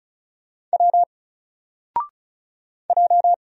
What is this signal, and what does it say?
Morse code, heard as a keyed tone, about 36 wpm, 700 Hz. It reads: W J
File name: Single-Letter-Number-35wpm